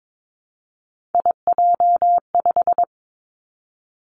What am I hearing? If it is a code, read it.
Morse code, heard as a keyed tone, about 22 wpm, 700 Hz. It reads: IJ5
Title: Three Challenging Letter-Number Combinations 22wpm